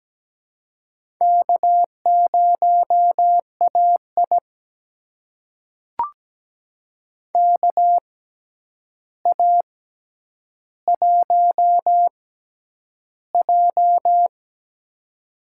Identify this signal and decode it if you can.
Morse code, heard as a keyed tone, about 17 wpm, spaced as for 9 wpm, 700 Hz. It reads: K0AI KA1J
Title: USA Call Signs - 4 Characters - 6x Character Spacing - 17wpm